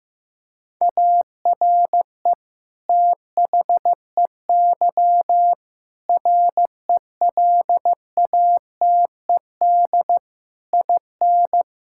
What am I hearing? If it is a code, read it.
Morse code, heard as a keyed tone, about 16 wpm, 700 Hz. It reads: ARE THEY RELATED IN
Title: Sentences from Top 1000 Words - Review Part 5 - 15wpm